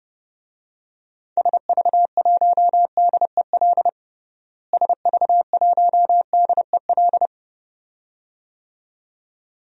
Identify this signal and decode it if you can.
Morse code, heard as a keyed tone, about 30 wpm, 700 Hz. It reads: SV1DEL SV1DEL
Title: DXCC-100 Call Signs - 6 Characters - Repeated 2x - 30wpm